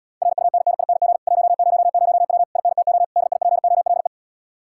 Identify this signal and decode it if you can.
Morse code, heard as a keyed tone, about 81 wpm, 700 Hz. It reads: RUNNING 100W INTO DIPOLE